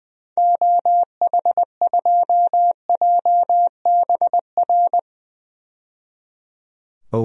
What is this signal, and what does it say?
Morse code, heard as a keyed tone, about 20 wpm, 700 Hz. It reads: OH2JBR
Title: DXCC-100 Call Signs - 6 Characters - 20wpm